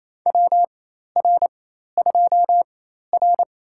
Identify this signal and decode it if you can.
Morse code, heard as a keyed tone, about 28 wpm, spaced as for 13 wpm, 700 Hz. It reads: WR2R
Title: USA Call Signs - 4 Characters - 4x Character Spacing - 28wpm